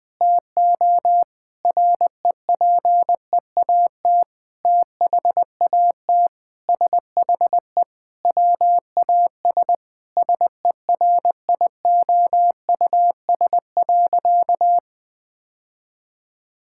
Morse code, heard as a keyed tone, about 20 wpm, 700 Hz. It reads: TO REPEAT THAT SHE WAS SERIOUS.
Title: Sentences from Top 1600 Words 20wpm